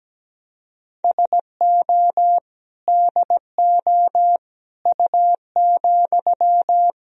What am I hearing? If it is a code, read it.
Morse code, heard as a keyed tone, about 17 wpm, 700 Hz. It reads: SO DO U,
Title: Sentences from Top 400 Words and Abbreviations 17wpm